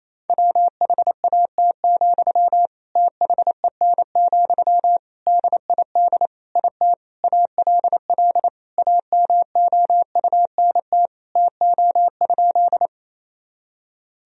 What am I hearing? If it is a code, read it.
Morse code, heard as a keyed tone, about 28 wpm, 700 Hz. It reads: WHAT, THEN, DID IT ALL AMOUNT TO?
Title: Sentences from Top 1000 Words - Review Part 4 - 28wpm